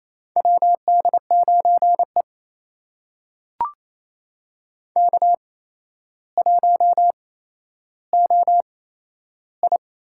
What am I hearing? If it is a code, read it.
Morse code, heard as a keyed tone, about 28 wpm, spaced as for 12 wpm, 700 Hz. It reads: WD9E K1OI
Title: USA Call Signs - 8x Character Spacing - 28wpm